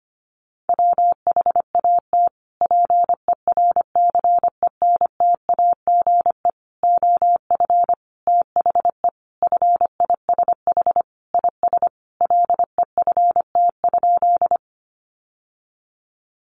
Morse code, heard as a keyed tone, about 25 wpm, 700 Hz. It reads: WHAT PERCENTAGE OF THE FISH IS LEFT?
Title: Sentences from Top 1300 Words 25wpm